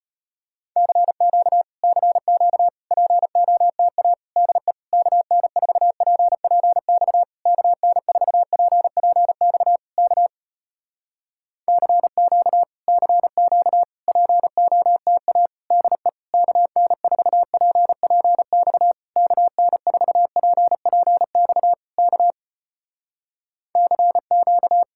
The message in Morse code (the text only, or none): CQ CQ POTA DE KN4PPX KN4PPX K CQ CQ POTA DE KN4PPX KN4PPX K CQ